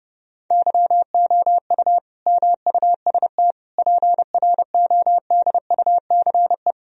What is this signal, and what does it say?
Morse code, heard as a keyed tone, about 30 wpm, 700 Hz. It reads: YOU MUST PRODUCE
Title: Sentences from Top 1000 Words - Review Part 3 - 30wpm